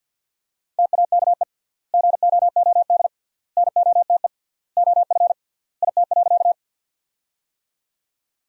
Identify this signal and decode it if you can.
Morse code, heard as a keyed tone, about 47 wpm, 700 Hz. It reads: TAKE GOOD NOTE OF IT.
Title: 3x Word-Spacing - Sentences from Top 500 Words - 50wpm